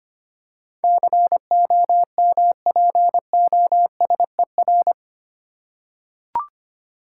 Morse code, heard as a keyed tone, about 25 wpm, 700 Hz. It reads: COMPOSER
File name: 8 Letter Words - First 3 Letters Given - 25wpm